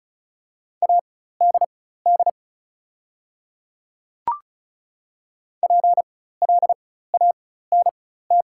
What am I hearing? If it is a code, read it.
Morse code, heard as a keyed tone, about 35 wpm, spaced as for 16 wpm, 700 Hz. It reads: ADD PLANT
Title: 4x Character Spacing - Top 300 Words - 35wpm